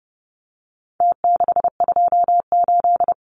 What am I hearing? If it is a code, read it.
Morse code, heard as a keyed tone, about 30 wpm, 700 Hz. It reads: T628
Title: Numbers - Contest Style - 4-Digit 30wpm